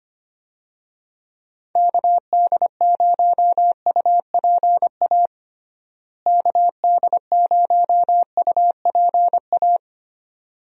Morse code, heard as a keyed tone, about 25 wpm, 700 Hz. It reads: KD0UPA KD0UPA
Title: USA Call Signs - 6 Characters - Repeated 2x - 25wpm